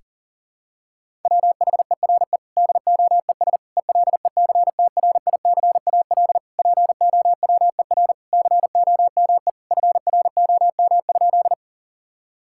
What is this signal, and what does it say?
Morse code, heard as a keyed tone, about 37 wpm, 700 Hz. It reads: WHERE DOES ELECTRICAL POWER COME FROM?
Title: Sentences from Top 1000 Words - Review Part 6 - 40wpm